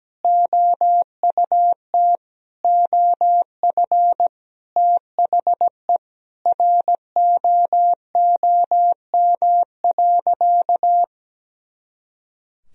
Morse code, heard as a keyed tone, about 17 wpm, 700 Hz. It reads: OUT OF THE ROOM.
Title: Sentences from Top 500 Words and Abbreviations 17wpm